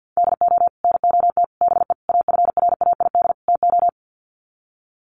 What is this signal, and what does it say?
Morse code, heard as a keyed tone, about 50 wpm, 700 Hz. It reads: DO NOT BE AFRAID TO
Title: Sets of 5 Words Encore2 50wpm